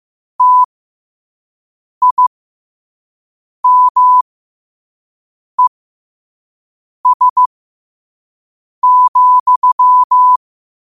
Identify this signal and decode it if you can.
Morse code, heard as a keyed tone, about 15 wpm, spaced as for 5 wpm, 1000 Hz. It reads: TIMES,
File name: qotdfiles / 5wpm